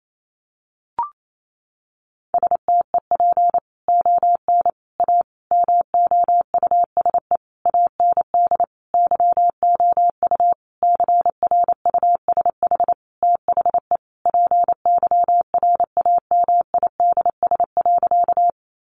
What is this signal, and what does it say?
Morse code, heard as a keyed tone, about 28 wpm, 700 Hz. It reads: STEP ON A MOUSE AND YOU CRUSH THE PYRAMIDS.
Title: A Sound Of Thunder 28wpm